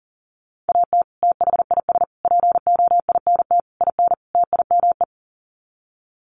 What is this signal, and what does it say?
Morse code, heard as a keyed tone, about 40 wpm, 700 Hz. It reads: AT THIS POINT IN TIME
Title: Sets of 5 Words Encore 40wpm